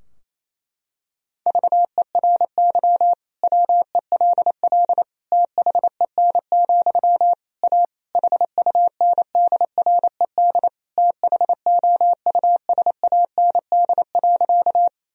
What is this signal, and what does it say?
Morse code, heard as a keyed tone, about 28 wpm, 700 Hz. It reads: VERY WELL THEN, A HUNDRED THOUSAND.
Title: Sentences from Top 1000 Words - Review Part 3 - 28wpm